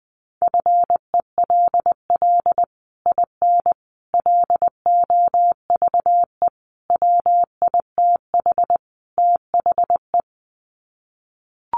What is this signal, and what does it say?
Morse code, heard as a keyed tone, about 20 wpm, 700 Hz. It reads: FELL IN LOVE WITH THE
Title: Sets of 5 Words Encore2 20wpm